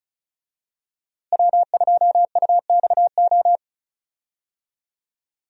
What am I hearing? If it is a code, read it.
Morse code, heard as a keyed tone, about 36 wpm, 700 Hz. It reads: W2UXO
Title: USA Call Signs - 5 Characters - Repeated 2x - 35wpm